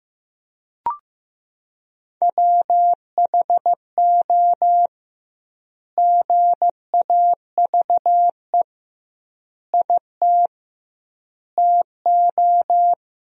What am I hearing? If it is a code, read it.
Morse code, heard as a keyed tone, about 15 wpm, 700 Hz. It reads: WHO GAVE IT TO
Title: 2x Word-Spacing - Sentences from Top 500 Words - 15wpm